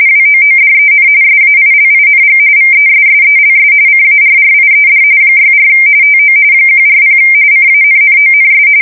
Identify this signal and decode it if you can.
RTTY, about 45.45 baud. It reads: WOODEN SAILING VESSELS CARRIED 64 TO 140 CANNONS
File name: fsk170 / USS-New-Jersey